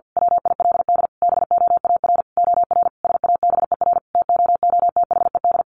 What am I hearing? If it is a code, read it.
Morse code, heard as a keyed tone, about 48 wpm, 700 Hz. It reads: WILD BOAR OR SABER TOOTHED
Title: A Sound Of Thunder 50wpm